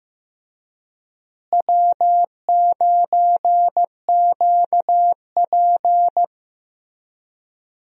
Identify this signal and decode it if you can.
Morse code, heard as a keyed tone, about 16 wpm, 700 Hz. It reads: W9QP
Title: USA Call Signs - 4 Characters - Repeated 3x - 15wpm